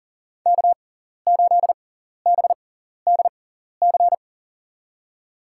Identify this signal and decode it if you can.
Morse code, heard as a keyed tone, about 40 wpm, spaced as for 14 wpm, 700 Hz. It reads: K8BDC
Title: USA Call Signs - 5 Characters - 6x Character Spacing - 40wpm